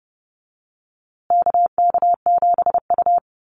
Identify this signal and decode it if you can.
Morse code, heard as a keyed tone, about 30 wpm, 700 Hz. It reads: KK7U
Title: USA Call Signs Repeated 2x 30wpm